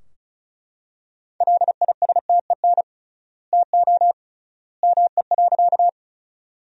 Morse code, heard as a keyed tone, about 36 wpm, 700 Hz. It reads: LISTEN TO ME.
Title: 3x Word-Spacing - Sentences from Top 400 Words - 35wpm